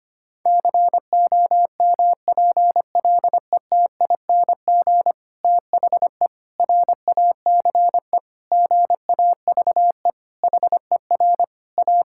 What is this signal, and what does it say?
Morse code, heard as a keyed tone, about 25 wpm, 700 Hz. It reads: COMPLETING THE RACE GAVE HER A
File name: practice / 25wpm / Difficult Sentences with Hints - 25wpm